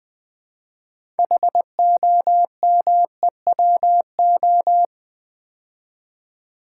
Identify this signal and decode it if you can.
Morse code, heard as a keyed tone, about 20 wpm, 700 Hz. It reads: HOMEWO
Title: Build A Word - 2 Letters At A Time - 8-Letter Words - 20wpm